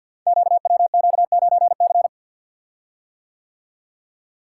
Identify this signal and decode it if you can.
Morse code, heard as a keyed tone, about 52 wpm, 700 Hz. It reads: QWQ9C
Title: Code Groups - 5 Letter:Numbers - 50wpm